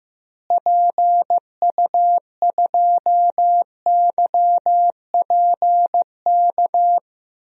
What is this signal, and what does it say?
Morse code, heard as a keyed tone, about 16 wpm, 700 Hz. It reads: PU2YPK
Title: DXCC-100 Call Signs - 6 Characters - Repeated 2x - 15wpm